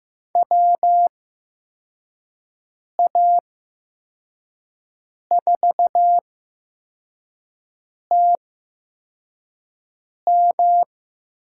Morse code, heard as a keyed tone, about 15 wpm, spaced as for 4 wpm, 700 Hz. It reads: WA4TM
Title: USA Call Signs - 8x Character Spacing - 15wpm